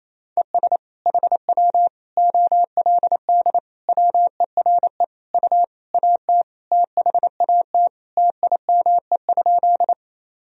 Morse code, heard as a keyed tone, about 28 wpm, 700 Hz. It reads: ES HW OLD WERE U AT THAT TIME?
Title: Sentences from Top 200 Words and Abbreviations 28wpm